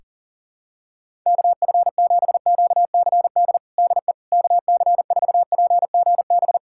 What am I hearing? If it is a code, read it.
Morse code, heard as a keyed tone, about 40 wpm, 700 Hz. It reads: KF7QCD DE KC4PGB
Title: USA Callsign DE Callsign 40wpm